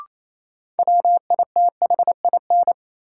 Morse code, heard as a keyed tone, about 28 wpm, 700 Hz. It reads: WITHIN
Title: Sentences from Top 700 Words - New Word Drill - 28wpm